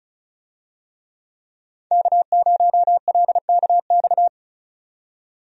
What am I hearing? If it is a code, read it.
Morse code, heard as a keyed tone, about 34 wpm, 700 Hz. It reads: K0LKX
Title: USA Call Signs - 5 Characters - 35wpm